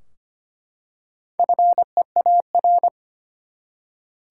Morse code, heard as a keyed tone, about 25 wpm, 700 Hz. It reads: FEAR
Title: Four Letter Words - 25wpm